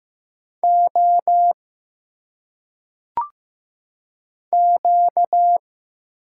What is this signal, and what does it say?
Morse code, heard as a keyed tone, about 15 wpm, 700 Hz. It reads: O Q
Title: Single Letters - Phonetic Pronunciation 15wpm